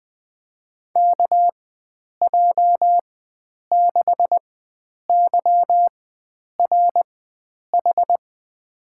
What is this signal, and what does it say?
Morse code, heard as a keyed tone, about 21 wpm, spaced as for 9 wpm, 700 Hz. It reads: KJ6YRH
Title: USA Call Signs - 6 Characters - 4x Character Spacing - 20wpm